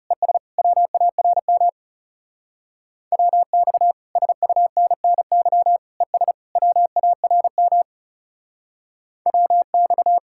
Morse code, heard as a keyed tone, about 36 wpm, 700 Hz. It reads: ES WARM WX SUNNY ES WARM WX
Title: Speed-Racing Common QSO Phrases - 30wpm